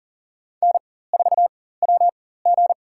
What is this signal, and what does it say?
Morse code, heard as a keyed tone, about 40 wpm, spaced as for 19 wpm, 700 Hz. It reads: N4WG
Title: USA Call Signs - 4 Characters - 4x Character Spacing - 40wpm